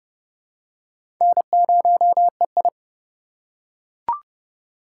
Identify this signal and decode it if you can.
Morse code, heard as a keyed tone, about 31 wpm, 700 Hz. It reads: N0EI
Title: USA Call Signs - 4x Character Spacing - 30wpm